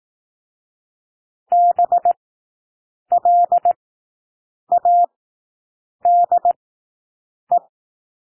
Morse code, heard as a keyed tone, about 18 wpm, spaced as for 7 wpm, 700 Hz. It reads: BLADE
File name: SOUND WORDS 18 wpm